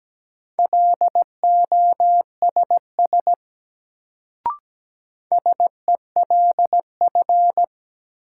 Morse code, heard as a keyed tone, about 17 wpm, 700 Hz. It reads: LOSS SELF
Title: Sentences from Top 1400 Words - New Word Drill Alt - 17wpm